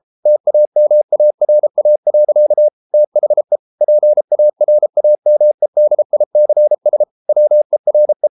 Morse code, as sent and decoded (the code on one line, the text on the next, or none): - .- -- .- .-. .- .-.-.- / - .... / .--. .- .-. .- -- . -.. .. -.-. ... / .-- . .-. .
TAMARA. THE PARAMEDICS WERE